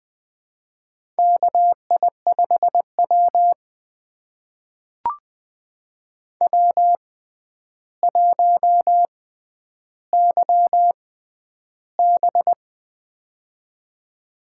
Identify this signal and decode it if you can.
Morse code, heard as a keyed tone, about 20 wpm, spaced as for 10 wpm, 700 Hz. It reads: KI5W W1YB
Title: USA Call Signs - 6x Character Spacing - 20wpm